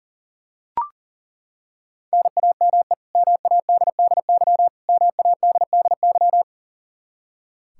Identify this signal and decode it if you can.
Morse code, heard as a keyed tone, about 40 wpm, 700 Hz. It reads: NAME MADDY MADDY
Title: Practice QSO - Exchange 1 - Beginning - 40wpm